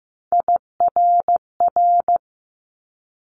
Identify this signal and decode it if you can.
Morse code, heard as a keyed tone, about 15 wpm, 700 Hz. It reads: IRR
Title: Multi-letter Phonemes 15wpm